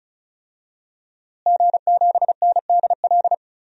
Morse code, heard as a keyed tone, about 34 wpm, 690 Hz. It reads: G7NDL
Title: DXCC-100 Call Signs - 5 Characters - Repeated 3x - 35wpm